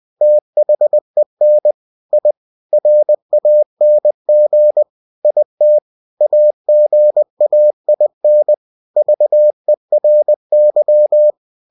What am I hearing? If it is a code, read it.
Morse code, heard as a keyed tone, about 20 wpm, 600 Hz. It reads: THEN I RANG IT AGAIN VERY
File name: cw-books / winnie-the-pooh / winnie-20wpm-page-0032